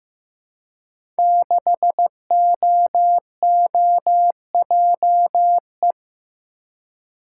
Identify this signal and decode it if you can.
Morse code, heard as a keyed tone, about 15 wpm, 700 Hz. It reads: 6OOJE
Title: Code Groups - 5 Letter:Numbers - 15wpm